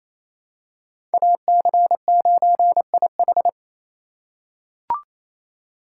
Morse code, heard as a keyed tone, about 28 wpm, 700 Hz. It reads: AC9IH
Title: USA Call Signs - 4x Character Spacing - 28wpm